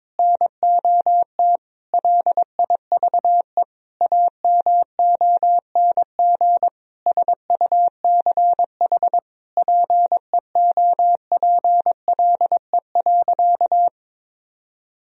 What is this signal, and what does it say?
Morse code, heard as a keyed tone, about 22 wpm, 700 Hz. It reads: NOT LIVE AMONG SUCH PEOPLE.
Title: Sentences from Top 1000 Words - Review Part 3 - 22wpm